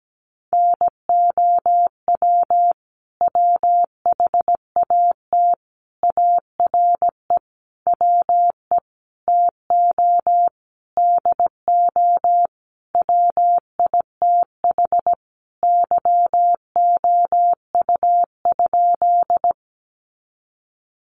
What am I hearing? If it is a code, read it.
Morse code, heard as a keyed tone, about 17 wpm, 700 Hz. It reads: NOW WHAT ARE WE TO DO WITH YOU?
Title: Sentences from Top 100 Words 17wpm